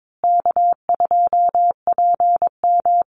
Morse code, heard as a keyed tone, about 22 wpm, 700 Hz. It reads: K2PM
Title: USA Call Signs Repeated 3x 22wpm